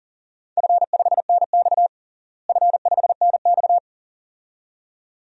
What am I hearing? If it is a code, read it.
Morse code, heard as a keyed tone, about 40 wpm, 700 Hz. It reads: F5NX F5NX
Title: DXCC-100 Call Signs - 4 Characters - Repeated 2x - 40wpm